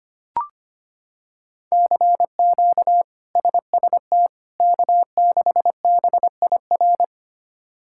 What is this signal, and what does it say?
Morse code, heard as a keyed tone, about 25 wpm, 700 Hz. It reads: CQ SST K6BIR
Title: K1USN SST QSO Practice - 25wpm